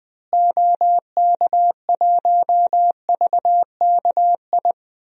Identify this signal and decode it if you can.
Morse code, heard as a keyed tone, about 21 wpm, 700 Hz. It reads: OK1VKI
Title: DXCC-100 Call Signs - 6 Characters - Repeated 2x - 20wpm